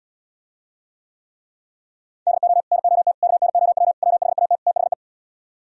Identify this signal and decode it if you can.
Morse code, heard as a keyed tone, about 78 wpm, 700 Hz. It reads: DO NOT KNOW WHAT THE